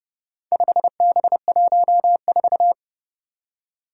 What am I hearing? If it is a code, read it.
Morse code, heard as a keyed tone, about 30 wpm, 700 Hz. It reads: 5B14
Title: Four Challenging Letter-Number Combinations 30wpm